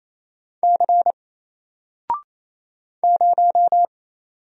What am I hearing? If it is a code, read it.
Morse code, heard as a keyed tone, about 27 wpm, 700 Hz. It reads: C 0
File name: Single-Letter-Number-28wpm